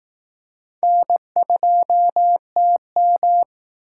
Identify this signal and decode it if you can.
Morse code, heard as a keyed tone, about 19 wpm, 700 Hz. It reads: N2TM